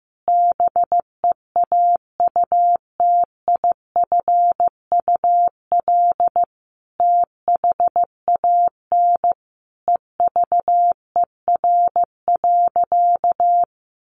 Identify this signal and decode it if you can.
Morse code, heard as a keyed tone, about 15 wpm, 700 Hz. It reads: BEAUTIFUL THAN EVER.
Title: Sentences from Top 1200 Words 15wpm